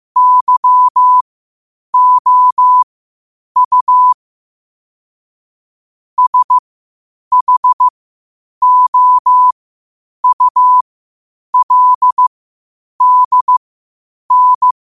Morse code, heard as a keyed tone, about 15 wpm, spaced as for 8 wpm, 1000 Hz. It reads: YOU SHOULDN
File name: qotdfiles / 8wpm